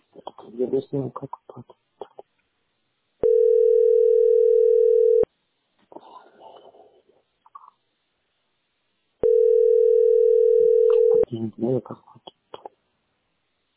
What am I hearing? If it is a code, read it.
Ringback tone (North America), 440 + 480 Hz, with speech.
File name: d8dec311-56b1-4ce5-81c9